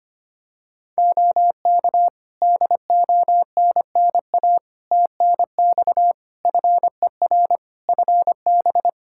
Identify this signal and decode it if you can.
Morse code, heard as a keyed tone, about 25 wpm, 700 Hz. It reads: OK DONNA TNX FER FB
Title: Practice QSO - Exchange 4 - Ending - 25wpm